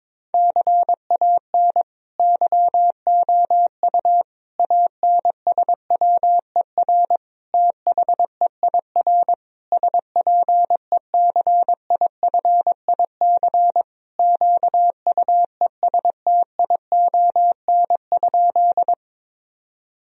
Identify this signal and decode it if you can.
Morse code, heard as a keyed tone, about 22 wpm, 700 Hz. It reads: CAN YOU ANSWER THEIR SPECIFIC QUESTION?
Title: Sentences from Top 1000 Words - Review Part 3 - 22wpm